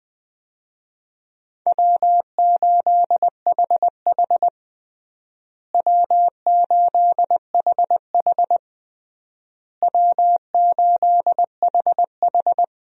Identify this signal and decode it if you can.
Morse code, heard as a keyed tone, about 20 wpm, 700 Hz. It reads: W8HH W8HH W8HH
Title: USA Call Signs - 4 Characters - Repeated 3x - 20wpm